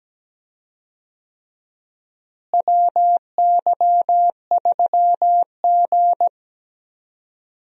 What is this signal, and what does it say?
Morse code, heard as a keyed tone, about 17 wpm, 700 Hz. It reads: WY3G
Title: USA Call Signs - 4 Characters - Repeated 3x - 17wpm